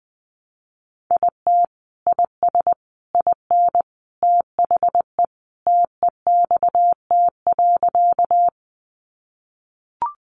Morse code, heard as a keyed tone, about 20 wpm, 700 Hz. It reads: IT IS IN THE TEXT.
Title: Sentences from Top 1200 Words 20wpm